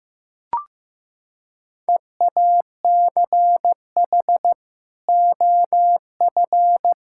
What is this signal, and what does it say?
Morse code, heard as a keyed tone, about 15 wpm, 700 Hz. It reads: EACH OF
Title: Sentences from Top 1000 Words - Review Part 5 - 15wpm